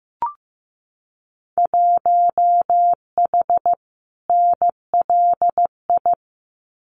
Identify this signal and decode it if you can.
Morse code, heard as a keyed tone, about 15 wpm, 700 Hz. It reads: 1H NLI
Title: Winter Field Day Exchanges 15wpm